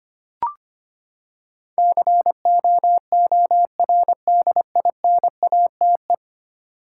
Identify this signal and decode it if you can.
Morse code, heard as a keyed tone, about 25 wpm, 700 Hz. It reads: COORDINATE
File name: Ten Letter Words - 25wpm